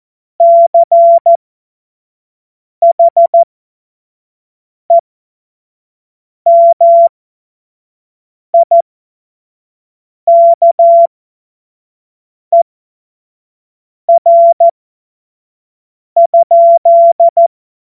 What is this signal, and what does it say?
Morse code, heard as a keyed tone, about 14 wpm, spaced as for 5 wpm, 680 Hz. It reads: CHEMIKER?